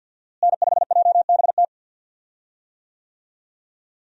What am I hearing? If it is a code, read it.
Morse code, heard as a keyed tone, about 48 wpm, 700 Hz. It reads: N5JBT
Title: USA Call Signs - 5 Characters - Repeated 3x - 50wpm